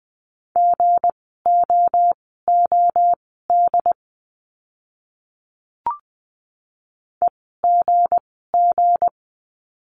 Morse code, heard as a keyed tone, about 20 wpm, spaced as for 14 wpm, 700 Hz. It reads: GOOD EGG
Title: 160 Lesson 26f - Letter g - Words 20wpm